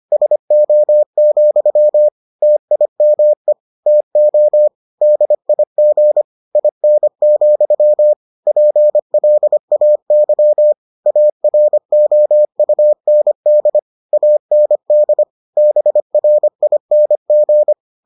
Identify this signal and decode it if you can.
Morse code, heard as a keyed tone, about 25 wpm, 600 Hz. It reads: SO, TIME TO DIG IN, PLAY AROUND AND BRING